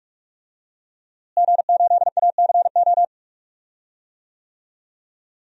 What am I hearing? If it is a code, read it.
Morse code, heard as a keyed tone, about 44 wpm, 700 Hz. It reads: G8ACO
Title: DXCC-100 Call Signs - 5 Characters - Repeated 3x - 45wpm